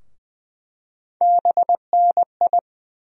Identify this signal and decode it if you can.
Morse code, heard as a keyed tone, about 20 wpm, 700 Hz. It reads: BNI
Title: USA Call Sign Postfixes - 3 Letters 20wpm